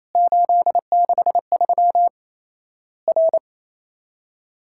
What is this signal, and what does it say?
Morse code, heard as a keyed tone, about 27 wpm, 700 Hz. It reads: 863 R
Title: Worked All Europe DX Contest - QTC Exchange - 28wpm